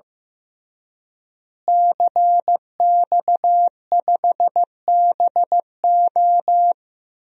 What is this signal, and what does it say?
Morse code, heard as a keyed tone, about 15 wpm, 700 Hz. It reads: CX5BO
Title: DXCC-100 Call Signs - 5 Characters - Repeated 3x - 15wpm